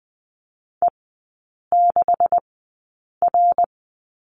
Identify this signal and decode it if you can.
Morse code, heard as a keyed tone, about 20 wpm, spaced as for 8 wpm, 700 Hz. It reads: E6R
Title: 191 Lesson 31e - Letter v - Three Characters 20wpm